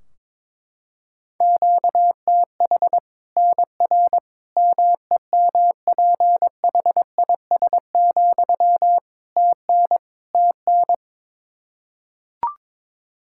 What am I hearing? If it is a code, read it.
Morse code, heard as a keyed tone, about 22 wpm, 700 Hz. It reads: QTH NR MEMPHIS, TN TN
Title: Practice QSO - Exchange 1 - Beginning - 22wpm